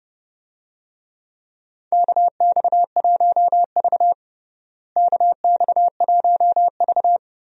Morse code, heard as a keyed tone, about 30 wpm, 700 Hz. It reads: KX1V KX1V
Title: USA Call Signs - 4 Characters - Repeated 3x - 30wpm